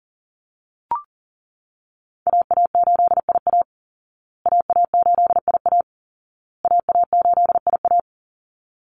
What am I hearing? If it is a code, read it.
Morse code, heard as a keyed tone, about 40 wpm, 700 Hz. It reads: AA8IA AA8IA AA8IA
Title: Call Signs Repeated 2x - 40wpm